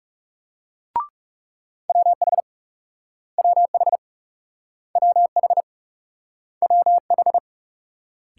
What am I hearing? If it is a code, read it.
Morse code, heard as a keyed tone, about 38 wpm, 700 Hz. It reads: WH WH WH WH